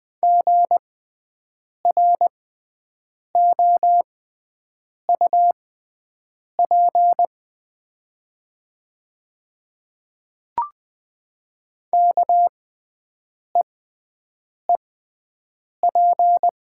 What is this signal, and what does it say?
Morse code, heard as a keyed tone, about 20 wpm, spaced as for 7 wpm, 700 Hz. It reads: GROUP KEEP
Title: 6x Character Spacing - Top 300 Words - 20wpm